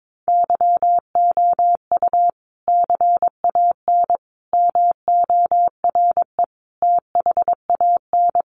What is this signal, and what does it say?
Morse code, heard as a keyed tone, about 22 wpm, 700 Hz. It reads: YOU CAN MORE THAN
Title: Sentences from Top 1400 Words 22wpm